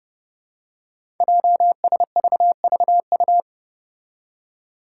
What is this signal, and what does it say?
Morse code, heard as a keyed tone, about 30 wpm, 700 Hz. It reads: JSVVU
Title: Five Challenging Letter-Number Combinations 30wpm